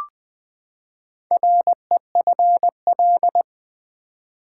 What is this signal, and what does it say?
Morse code, heard as a keyed tone, about 20 wpm, 700 Hz. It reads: REFL
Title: Build A Word - 4 Letters At A Time - 10-Letter Words - 20wpm